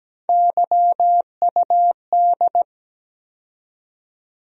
Morse code, heard as a keyed tone, about 17 wpm, 700 Hz. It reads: YUD
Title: USA Call Sign Postfixes - 3 Letters 17wpm